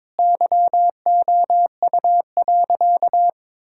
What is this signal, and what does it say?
Morse code, heard as a keyed tone, about 22 wpm, 700 Hz. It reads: YOU.
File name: practice / 22wpm / Sentences with 8-Word Phrases - 22wpm